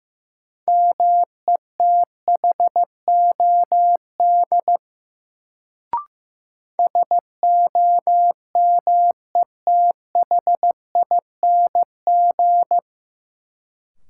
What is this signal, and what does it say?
Morse code, heard as a keyed tone, about 15 wpm, 700 Hz. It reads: METHOD SOMETHING
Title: Sentences from Top 600 Words - New Word Drill - 15wpm